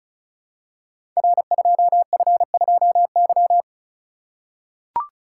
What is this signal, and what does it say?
Morse code, heard as a keyed tone, about 36 wpm, 700 Hz. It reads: R2F2Y
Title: Code Groups - 5 Letter:Numbers - 35wpm